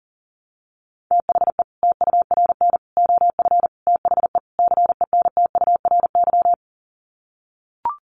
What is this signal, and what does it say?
Morse code, heard as a keyed tone, about 40 wpm, 700 Hz. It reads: THE TURN OF THE CENTURY
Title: Sets of 5 Words 40wpm